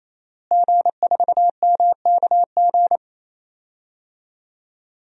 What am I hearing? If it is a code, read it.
Morse code, heard as a keyed tone, about 28 wpm, 700 Hz. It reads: G4MKG
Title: DXCC-100 Call Signs - 5 Characters - Repeated 3x - 28wpm